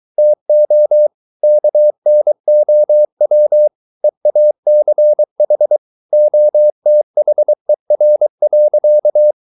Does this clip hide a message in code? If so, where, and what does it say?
Yes: Morse code from 0.2 to 9.3 s, heard as a keyed tone, about 23 wpm, 600 Hz. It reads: TO KNOW EACH OTHER.